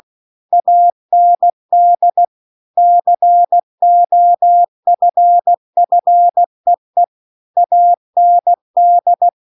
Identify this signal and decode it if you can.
Morse code, heard as a keyed tone, about 16 wpm, 700 Hz. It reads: AND COFFEE AND